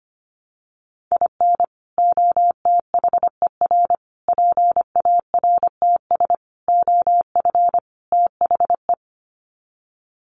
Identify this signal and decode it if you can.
Morse code, heard as a keyed tone, about 25 wpm, 700 Hz. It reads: IN OTHER PARTS OF THE
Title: Sets of 5 Words 25wpm